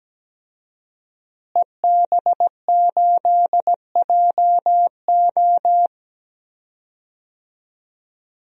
Morse code, heard as a keyed tone, about 17 wpm, 700 Hz. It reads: EB8JO
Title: DXCC-100 Call Signs - 5 Characters - Repeated 2x - 17wpm